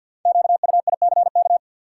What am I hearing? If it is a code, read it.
Morse code, heard as a keyed tone, about 53 wpm, 700 Hz. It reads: QUICK
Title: Sentences from Top 500 Words - New Word Drill - 50wpm